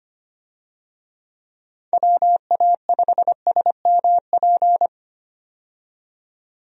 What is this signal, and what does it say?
Morse code, heard as a keyed tone, about 25 wpm, 700 Hz. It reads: WA5SMP
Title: USA Call Signs - 6 Characters - 25wpm